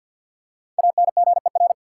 Morse code, heard as a keyed tone, about 53 wpm, 700 Hz. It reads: ANGER
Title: Five Letter Words - 50wpm